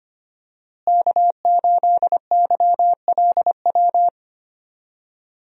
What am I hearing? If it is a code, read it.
Morse code, heard as a keyed tone, about 25 wpm, 700 Hz. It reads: K8YLW
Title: USA Call Signs - 5 Characters - Repeated 3x - 25wpm